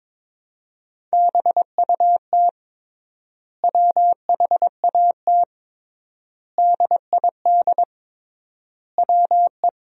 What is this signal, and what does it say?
Morse code, heard as a keyed tone, about 22 wpm, 700 Hz. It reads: BUT WHAT DID WE
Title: 3x Word-Spacing - Sentences from Top 400 Words - 22wpm